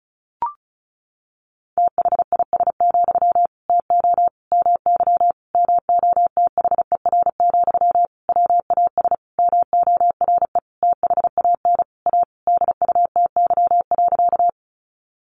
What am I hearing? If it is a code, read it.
Morse code, heard as a keyed tone, about 34 wpm, 700 Hz. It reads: THIS, TO MY MOTHER, WAS MORE THAN A DUTY.
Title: Sentences from Top 1500 Words 35wpm